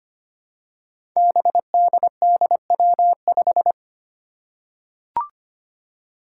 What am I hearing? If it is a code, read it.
Morse code, heard as a keyed tone, about 25 wpm, 700 Hz. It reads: BDDW5
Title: Five Challenging Letter-Number Combinations 25wpm